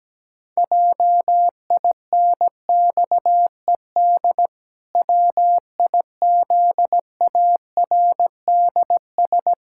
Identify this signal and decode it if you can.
Morse code, heard as a keyed tone, about 17 wpm, 700 Hz. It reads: JINXED WIZARDS
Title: Pangram Challenge v2 - 17wpm